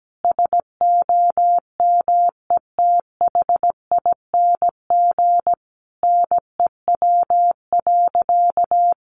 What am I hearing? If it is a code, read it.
Morse code, heard as a keyed tone, about 17 wpm, 700 Hz. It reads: SOMETHING NEW.
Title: Sentences from Top 1100 Words 17wpm